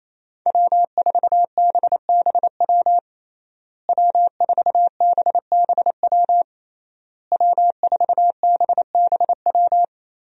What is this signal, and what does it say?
Morse code, heard as a keyed tone, about 28 wpm, 700 Hz. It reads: W4BBW W4BBW W4BBW
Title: USA Call Signs - 5 Characters - Repeated 3x - 28wpm